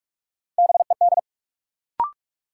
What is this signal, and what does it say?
Morse code, heard as a keyed tone, about 46 wpm, 700 Hz. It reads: BED
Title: Sets of 5 Words Encore - New Word Drill - 45wpm